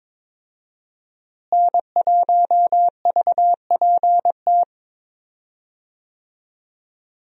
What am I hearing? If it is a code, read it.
Morse code, heard as a keyed tone, about 22 wpm, 690 Hz. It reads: N1VPT
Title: USA Call Signs - 5 Characters - Repeated 3x - 22wpm